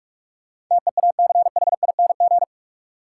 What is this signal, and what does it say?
Morse code, heard as a keyed tone, about 46 wpm, 700 Hz. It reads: TEACHING
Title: Sentences from Top 1100 Words - New Word Drill Alt - 45wpm